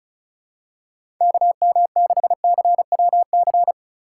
Morse code, heard as a keyed tone, about 36 wpm, 700 Hz. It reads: KM6CWC
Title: USA Call Signs - 6 Characters - 2x Character Spacing - 35wpm